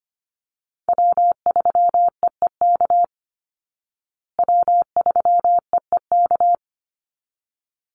Morse code, heard as a keyed tone, about 25 wpm, 700 Hz. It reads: W3EEK W3EEK
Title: Call Signs Repeated 3x - 25wpm